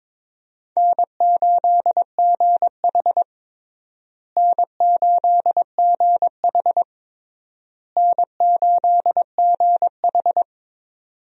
Morse code, heard as a keyed tone, about 22 wpm, 700 Hz. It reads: N8GH N8GH N8GH
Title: USA Call Signs - 4 Characters - Repeated 3x - 22wpm